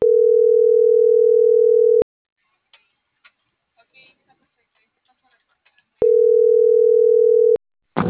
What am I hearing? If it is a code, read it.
Ringback tone (North America), 440 + 480 Hz.